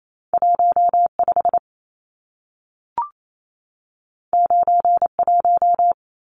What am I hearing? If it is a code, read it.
Morse code, heard as a keyed tone, about 28 wpm, 700 Hz. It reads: 15 91
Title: Numbers - 2-Digit 28wpm